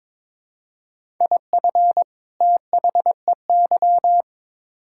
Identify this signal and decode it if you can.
Morse code, heard as a keyed tone, about 22 wpm, 700 Hz. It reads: IF THEY
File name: Build A Phrase - 2-Word Phrases - 22wpm